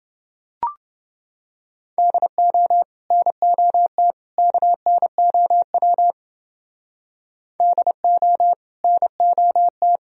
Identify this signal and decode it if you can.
Morse code, heard as a keyed tone, about 29 wpm, 700 Hz. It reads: DO NOT KNOW DO NOT